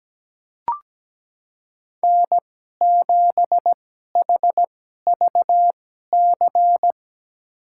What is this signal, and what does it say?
Morse code, heard as a keyed tone, about 17 wpm, spaced as for 12 wpm, 700 Hz. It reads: N7HVC
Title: USA Call Signs - 2x Character Spacing - 17wpm